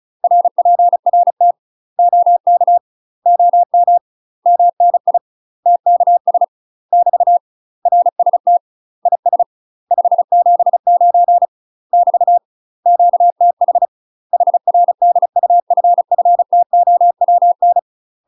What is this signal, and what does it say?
Morse code, heard as a keyed tone, about 35 wpm, 700 Hz. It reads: RPRT OK OM MNI TKS <BT> RST IS 579 <BT> QTH HR DUFFTOWN
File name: UK MASTER QSOs TRAINING / 35WPM / UK MASTER QSOs 0031